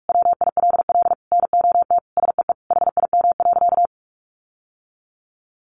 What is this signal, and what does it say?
Morse code, heard as a keyed tone, about 44 wpm, 700 Hz. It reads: WILL NOT SEE HIM.
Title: Sentences from Top 600 Words 45wpm